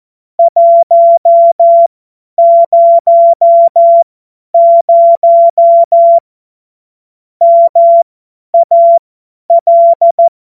Morse code, heard as a keyed tone, about 14 wpm, spaced as for 10 wpm, 680 Hz. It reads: 100 MAL
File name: Morsekurs / Witze / Witz 13. September 2022 50 F70